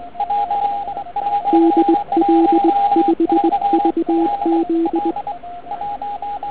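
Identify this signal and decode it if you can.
Morse code, heard as a keyed tone, about 20 wpm, 330 Hz, with radio noise. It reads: DL5VZ
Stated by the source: Morning 80m pile-up recorded at nearly the same time on three transceivers: FT1000MKV, FT817 and ATS-3B.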